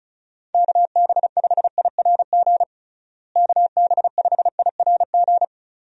Morse code, heard as a keyed tone, about 36 wpm, 700 Hz. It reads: KB5IRG KB5IRG
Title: USA Call Signs - 6 Characters - Repeated 3x - 35wpm